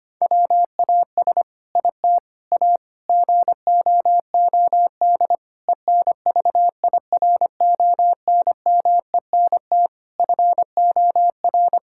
Morse code, heard as a keyed tone, about 25 wpm, 700 Hz. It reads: WAS IT A GOOD ENVIRONMENT FOR
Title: Sentences from Top 1000 Words - Review Part 5 - 25wpm